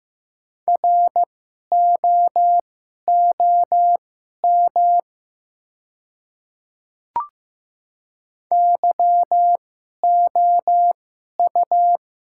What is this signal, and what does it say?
Morse code, heard as a keyed tone, about 15 wpm, spaced as for 11 wpm, 700 Hz. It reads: ROOM YOU
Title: 2x Character Spacing - Top 300 Words - 15wpm